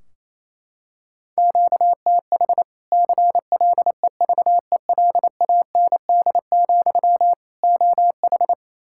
Morse code, heard as a keyed tone, about 28 wpm, 700 Hz. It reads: QTH CLEVELAND, OH
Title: Practice QSO - Exchange 1 - Beginning - 28wpm